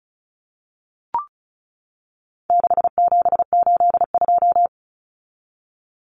Morse code, heard as a keyed tone, about 34 wpm, 700 Hz. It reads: 6782
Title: Common Cut Numbers - 4-Digit 35wpm